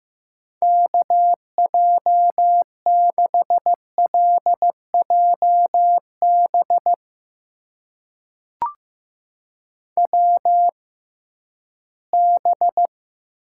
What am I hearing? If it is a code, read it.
Morse code, heard as a keyed tone, about 15 wpm, spaced as for 11 wpm, 700 Hz. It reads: KJ6LJB WB
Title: USA Call Signs - 6 Characters - 6x Character Spacing - 15wpm